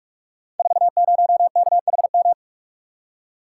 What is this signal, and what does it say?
Morse code, heard as a keyed tone, about 44 wpm, 700 Hz. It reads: 40KHM
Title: Code Groups - 5 Letter:Numbers - 45wpm